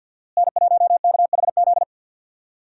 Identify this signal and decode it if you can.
Morse code, heard as a keyed tone, about 51 wpm, 700 Hz. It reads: N1KHZ
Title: USA Call Signs - 5 Characters - Repeated 2x - 50wpm